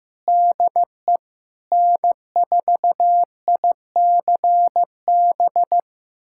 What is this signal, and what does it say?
Morse code, heard as a keyed tone, about 15 wpm, 700 Hz. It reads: DE N4ICB
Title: USA Callsign DE Callsign 15wpm